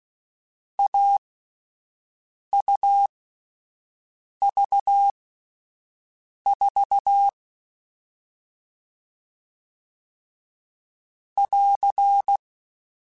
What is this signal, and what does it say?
Morse code, heard as a keyed tone, about 16 wpm, spaced as for 5 wpm, 800 Hz. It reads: AUV4 <AR>
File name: Difficult Combos / Difficult Combos-mp3s / AUV AUV4 @5